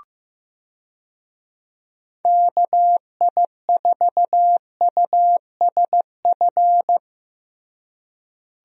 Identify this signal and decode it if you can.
Morse code, heard as a keyed tone, about 15 wpm, 700 Hz. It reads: KI4USF
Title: USA Call Signs - 6 Characters - Repeated 2x - 15wpm